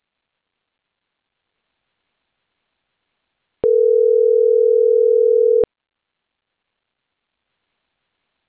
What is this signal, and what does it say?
Ringback tone (North America), 440 + 480 Hz.